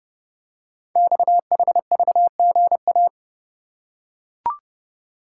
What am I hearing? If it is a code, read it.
Morse code, heard as a keyed tone, about 30 wpm, 700 Hz. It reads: XHVGA
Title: Code Groups - 5 Letters - 30wpm